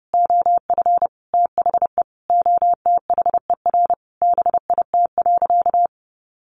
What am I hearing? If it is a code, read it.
Morse code, heard as a keyed tone, about 30 wpm, 700 Hz. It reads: OF THE OTHER BIT.
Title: Sentences from Top 900 Words 30wpm